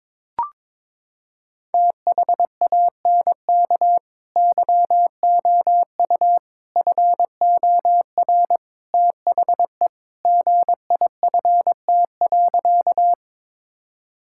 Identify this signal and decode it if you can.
Morse code, heard as a keyed tone, about 22 wpm, 700 Hz. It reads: THANK YOU FOR THE GIFT.
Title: Sentences from Top 1000 Words - Review Part 6 - 22wpm